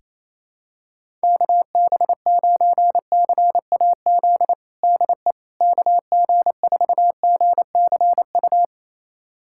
Morse code, heard as a keyed tone, about 28 wpm, 700 Hz. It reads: KB9CAZ DE KG4GCU
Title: USA Callsign DE Callsign 28wpm